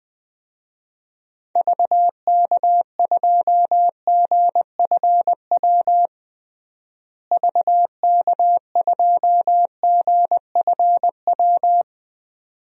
Morse code, heard as a keyed tone, about 20 wpm, 700 Hz. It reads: VK2GFW VK2GFW
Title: DXCC-100 Call Signs - 6 Characters - Repeated 2x - 20wpm